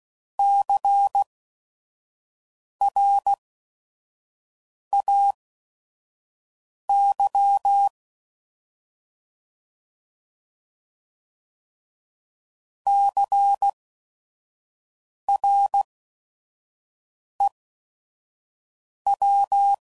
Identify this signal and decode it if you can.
Morse code, heard as a keyed tone, about 16 wpm, spaced as for 5 wpm, 800 Hz. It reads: CRAY CREW
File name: Prefixes / By First Letter / Cs